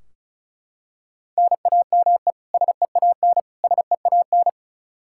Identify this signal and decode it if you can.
Morse code, heard as a keyed tone, about 36 wpm, 700 Hz. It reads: NAME SEAN SEAN
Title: Practice QSO - Exchange 1 - Beginning - 35wpm